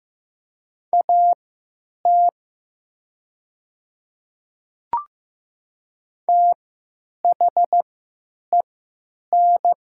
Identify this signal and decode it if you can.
Morse code, heard as a keyed tone, about 15 wpm, spaced as for 9 wpm, 700 Hz. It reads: AT THEN
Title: 3x Character Spacing - Top 100 Words - 15wpm